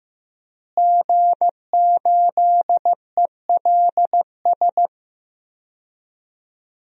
Morse code, heard as a keyed tone, about 15 wpm, 700 Hz. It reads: G8ELS
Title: DXCC-100 Call Signs - 5 Characters - Repeated 2x - 15wpm